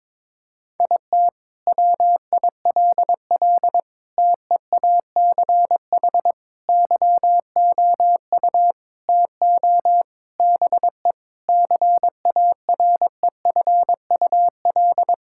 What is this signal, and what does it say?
Morse code, heard as a keyed tone, about 22 wpm, 700 Hz. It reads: IT WILL TEACH YOU TO BE CAREFUL
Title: Sentences from Top 1600 Words 22wpm